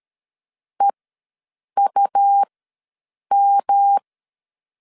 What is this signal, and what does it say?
Morse code, heard as a keyed tone, about 13 wpm, spaced as for 7 wpm, 800 Hz. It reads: EUM
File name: LEZIONI / 10 SIGNS / 10B INT ws 12wpm 3527